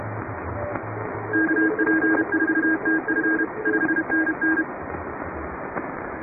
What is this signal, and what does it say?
Morse code, heard as a keyed tone, about 31 wpm, 1600 Hz, with radio noise. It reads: KP4TF 5NN
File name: chatham / 2350-12CW